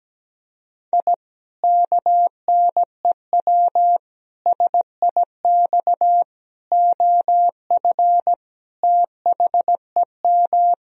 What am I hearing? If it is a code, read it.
Morse code, heard as a keyed tone, about 17 wpm, 700 Hz. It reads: I KNEW SIX OF THEM
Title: Sentences from Top 1000 Words - Review Part 2 - 17wpm